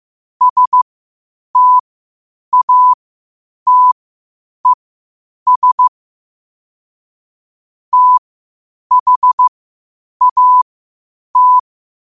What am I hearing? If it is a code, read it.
Morse code, heard as a keyed tone, about 15 wpm, spaced as for 8 wpm, 1000 Hz. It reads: STATES THAT
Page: New quotes every day in morse code at 8 Words per minute.